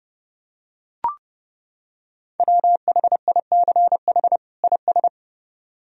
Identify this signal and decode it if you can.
Morse code, heard as a keyed tone, about 29 wpm, 700 Hz. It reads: WHICH IS